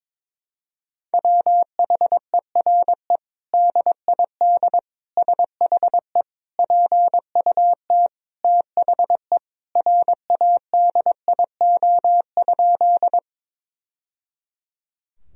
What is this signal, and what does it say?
Morse code, heard as a keyed tone, about 22 wpm, 700 Hz. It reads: WHERE DID SHE PUT THE RADIO?
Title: Sentences from Top 1000 Words - Review Part 3 - 22wpm